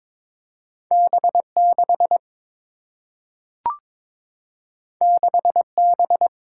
Morse code, heard as a keyed tone, about 21 wpm, 700 Hz. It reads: B 6 6 B
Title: Single-Letter Single-Number 22wpm